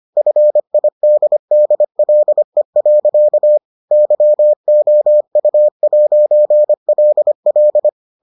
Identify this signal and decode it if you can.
Morse code, heard as a keyed tone, about 25 wpm, 600 Hz. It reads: FIDDLE. YOU'LL